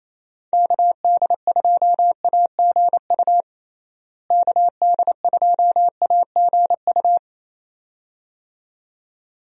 Morse code, heard as a keyed tone, about 28 wpm, 700 Hz. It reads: KD2AGU KD2AGU
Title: USA Call Signs - 6 Characters - Repeated 2x - 28wpm